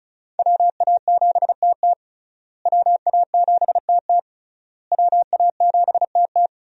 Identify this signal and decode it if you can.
Morse code, heard as a keyed tone, about 34 wpm, 700 Hz. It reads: WA7TT WA7TT WA7TT
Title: USA Call Signs - 5 Characters - Repeated 3x - 35wpm